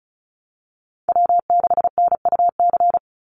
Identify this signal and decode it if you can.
Morse code, heard as a keyed tone, about 34 wpm, 700 Hz. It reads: W6NUC
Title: Call Signs Repeated 3x - 35wpm